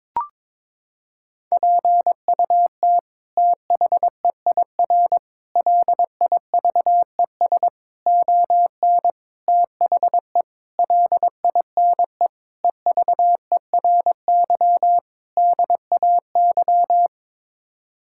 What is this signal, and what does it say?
Morse code, heard as a keyed tone, about 22 wpm, 700 Hz. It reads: PUT THEIR LIVES ON THE LINE EVERY DAY
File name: practice / 22wpm / Sets of 8 Words - 22wpm